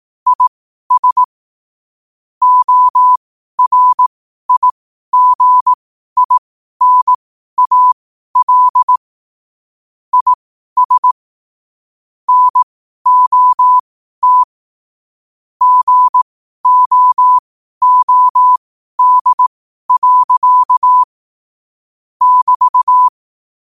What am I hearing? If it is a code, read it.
Morse code, heard as a keyed tone, about 18 wpm, spaced as for 12 wpm, 1000 Hz. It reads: IS ORIGINAL IS NOT GOOD. <BT>
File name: qotdfiles / 12wpm